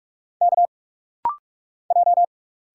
Morse code, heard as a keyed tone, about 46 wpm, 700 Hz. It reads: K J
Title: Single Letters - Rapid-Fire 45wpm